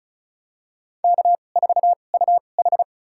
Morse code, heard as a keyed tone, about 36 wpm, spaced as for 25 wpm, 700 Hz. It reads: K4UH
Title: USA Call Signs - 4 Characters - 2x Character Spacing - 35wpm